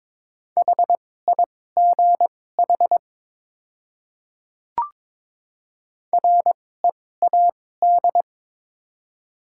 Morse code, heard as a keyed tone, about 22 wpm, spaced as for 16 wpm, 700 Hz. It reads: HIGH READ
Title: 2x Character Spacing - Top 300 Words - 22wpm